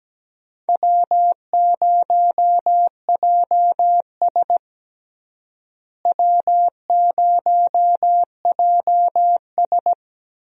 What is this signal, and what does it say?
Morse code, heard as a keyed tone, about 17 wpm, 700 Hz. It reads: W0JS W0JS
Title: USA Call Signs - 4 Characters - Repeated 3x - 17wpm